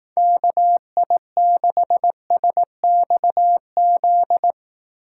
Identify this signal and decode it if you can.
Morse code, heard as a keyed tone, about 18 wpm, 700 Hz. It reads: KI6SXZ
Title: Speed Racing - USA Call Signs 15wpm